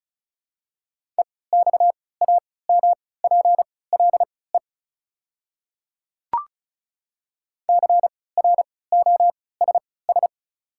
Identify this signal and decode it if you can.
Morse code, heard as a keyed tone, about 34 wpm, spaced as for 20 wpm, 700 Hz. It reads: EXAMPLE CROSS
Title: 3x Character Spacing - Top 500 Words - 35wpm